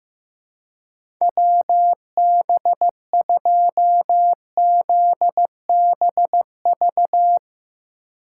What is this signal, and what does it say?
Morse code, heard as a keyed tone, about 15 wpm, 700 Hz. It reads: WB2ZBV
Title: USA Call Signs - 6 Characters - 2x Character Spacing - 15wpm